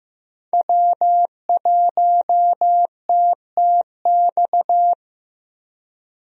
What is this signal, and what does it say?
Morse code, heard as a keyed tone, about 15 wpm, 700 Hz. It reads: W1TTX
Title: USA Call Signs - 5 Characters - Repeated 2x - 15wpm